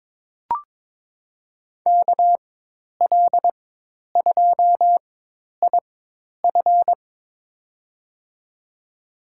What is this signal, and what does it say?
Morse code, heard as a keyed tone, about 23 wpm, spaced as for 10 wpm, 700 Hz. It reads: KL2IF
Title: USA Call Signs - 4x Character Spacing - 22wpm